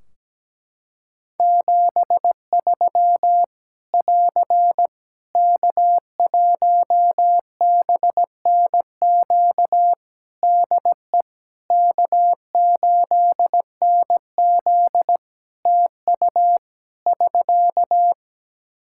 Morse code, heard as a keyed tone, about 17 wpm, 700 Hz. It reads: 73 <AR> K1BNQ DE K8NZ TU <SK>
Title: Practice QSO - Exchange 4 - Ending - 17wpm